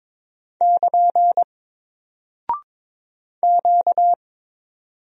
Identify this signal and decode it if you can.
Morse code, heard as a keyed tone, about 22 wpm, 700 Hz. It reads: <KN> Q
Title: Single Character (Letter:Number:Punctuation:Prosign) Alt 22wpm